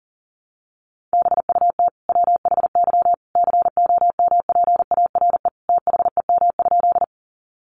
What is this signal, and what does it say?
Morse code, heard as a keyed tone, about 40 wpm, 700 Hz. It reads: BUT WHY COMPARE THEM?
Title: Sentences from Top 1300 Words 40wpm